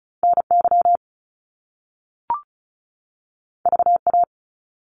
Morse code, heard as a keyed tone, about 36 wpm, 700 Hz. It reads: NY VA
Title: US States 35wpm